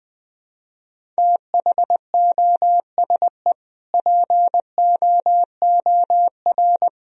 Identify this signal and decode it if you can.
Morse code, heard as a keyed tone, about 21 wpm, 700 Hz. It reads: THOSE POOR
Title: Sentences from Top 1000 Words - Review Part 1 - 20wpm